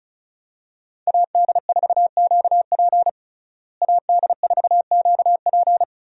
Morse code, heard as a keyed tone, about 36 wpm, 700 Hz. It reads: AD4QP AD4QP
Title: USA Call Signs - 5 Characters - Repeated 3x - 35wpm